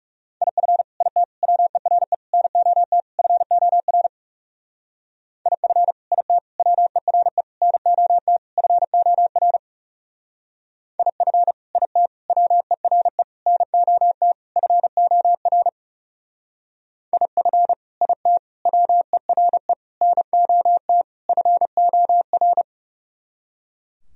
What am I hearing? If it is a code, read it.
Morse code, heard as a keyed tone, about 36 wpm, 700 Hz. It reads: IF IT WERE NOT FOR IF IT WERE NOT FOR IF IT WERE NOT FOR IF IT WERE NOT FOR
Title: Speed-Racing - Sets of 5 Words 30wpm